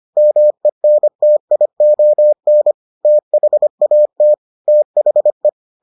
Morse code, heard as a keyed tone, about 25 wpm, 600 Hz. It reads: MENTION THAT THE